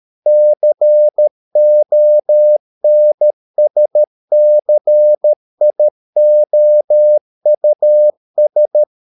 Morse code, heard as a keyed tone, about 13 wpm, 600 Hz. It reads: CONSCIOUS